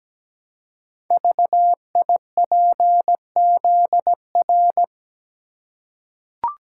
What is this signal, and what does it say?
Morse code, heard as a keyed tone, about 17 wpm, 700 Hz. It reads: VIPZR
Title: Code Groups - 5 Letters - 17wpm